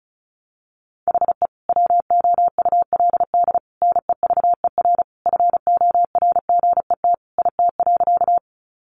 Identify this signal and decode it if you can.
Morse code, heard as a keyed tone, about 34 wpm, 700 Hz. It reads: HE WOULD NEVER FORGET IT.
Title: Sentences from Top 1200 Words 35wpm